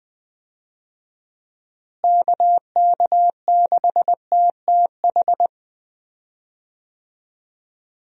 Morse code, heard as a keyed tone, about 20 wpm, 700 Hz. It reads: KK6TTH
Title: USA Call Signs - 6 Characters - Repeated 2x - 20wpm